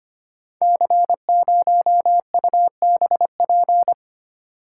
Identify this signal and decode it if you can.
Morse code, heard as a keyed tone, about 25 wpm, 700 Hz. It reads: C0UBP
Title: Code Groups - 5 Letter:Numbers - 25wpm